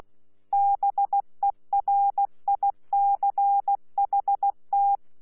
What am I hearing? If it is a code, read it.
Morse code, heard as a keyed tone, about 16 wpm, 800 Hz. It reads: BERICHT
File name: Z PI4VRZ / PI4VRZ 14-jun-2025